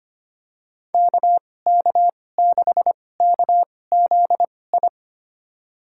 Morse code, heard as a keyed tone, about 25 wpm, spaced as for 18 wpm, 700 Hz. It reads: KK6KZI
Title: USA Call Signs - 6 Characters - 2x Character Spacing - 25wpm